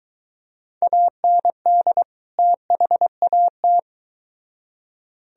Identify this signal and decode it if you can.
Morse code, heard as a keyed tone, about 23 wpm, 700 Hz. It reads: AND THAT